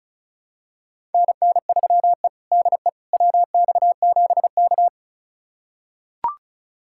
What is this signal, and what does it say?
Morse code, heard as a keyed tone, about 36 wpm, 700 Hz. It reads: NN3E DE WX7K
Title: USA Callsign DE Callsign - 4 Characters - 35wpm